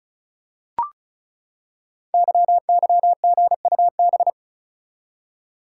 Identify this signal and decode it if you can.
Morse code, heard as a keyed tone, about 36 wpm, 690 Hz. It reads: YYGUB
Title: Code Groups - 5 Letters - 35wpm